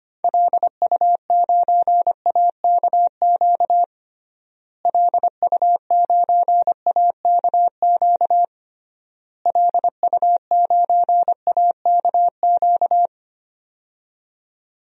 Morse code, heard as a keyed tone, about 25 wpm, 700 Hz. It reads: LU9AKQ LU9AKQ LU9AKQ
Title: DXCC-100 Call Signs - 6 Characters - Repeated 3x - 25wpm